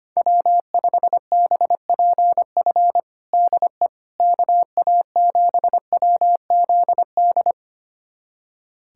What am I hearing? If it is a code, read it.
Morse code, heard as a keyed tone, about 25 wpm, 700 Hz. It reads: W5BPF DE KA7WZD
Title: USA Callsign DE Callsign 25wpm